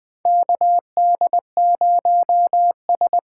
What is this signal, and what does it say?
Morse code, heard as a keyed tone, about 20 wpm, 700 Hz. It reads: KD0S
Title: USA Call Signs - 6 Characters - 8x Character Spacing - 20wpm